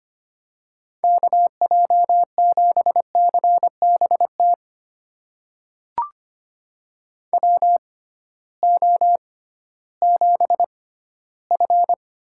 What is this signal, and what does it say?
Morse code, heard as a keyed tone, about 25 wpm, spaced as for 14 wpm, 700 Hz. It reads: KJ7CBT WO7F
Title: USA Call Signs - 6x Character Spacing - 25wpm